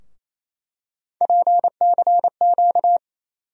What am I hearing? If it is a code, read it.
Morse code, heard as a keyed tone, about 27 wpm, 700 Hz. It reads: PCQ
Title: USA Call Sign Postfixes - 3 Letters 28wpm